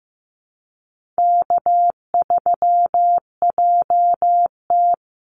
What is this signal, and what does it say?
Morse code, heard as a keyed tone, about 15 wpm, 700 Hz. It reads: K3JT
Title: Easy Call Signs Repeated 2x - 15wpm